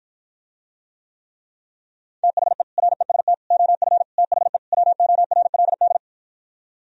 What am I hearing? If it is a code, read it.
Morse code, heard as a keyed tone, about 52 wpm, 700 Hz. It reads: THE REST OF THE WORLD